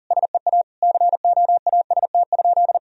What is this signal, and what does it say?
Morse code, heard as a keyed tone, about 40 wpm, 700 Hz. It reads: SEA COAST?
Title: Sentences from Top 1000 Words - Review Part 4 - 40wpm